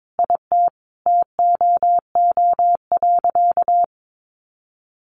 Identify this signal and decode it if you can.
Morse code, heard as a keyed tone, about 22 wpm, 700 Hz. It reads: IT TOO.
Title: Additional Easy Phrases 22wpm